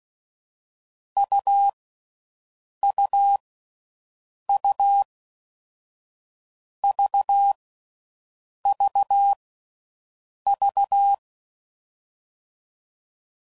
Morse code, heard as a keyed tone, about 16 wpm, 800 Hz. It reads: U U U V V V
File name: Combinations / Difficult Combos / AUV AUV4